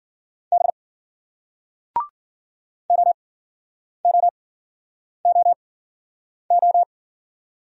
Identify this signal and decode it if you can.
Morse code, heard as a keyed tone, about 54 wpm, spaced as for 9 wpm, 700 Hz. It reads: B OOOO